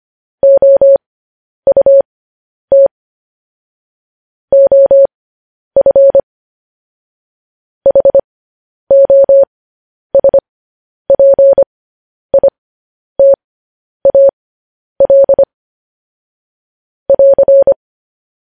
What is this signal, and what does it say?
Morse code, heard as a keyed tone, about 25 wpm, spaced as for 10 wpm, 550 Hz. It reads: OUT OF HOSPITAL <AR>
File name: podcast / News headlines-1x-25x10